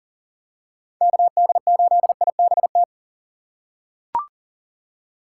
Morse code, heard as a keyed tone, about 40 wpm, 700 Hz. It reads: KD8IBT
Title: USA Call Signs - 4x Character Spacing - 40wpm